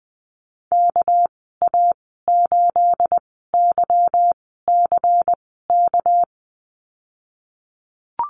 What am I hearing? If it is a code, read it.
Morse code, heard as a keyed tone, about 20 wpm, spaced as for 15 wpm, 700 Hz. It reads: KA8YCK
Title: 245 Lesson 39g - Letter z - Call Signs 20wpm